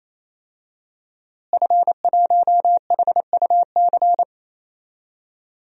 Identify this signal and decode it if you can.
Morse code, heard as a keyed tone, about 28 wpm, 700 Hz. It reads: F1HUC
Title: DXCC-100 Call Signs - 5 Characters - Repeated 3x - 28wpm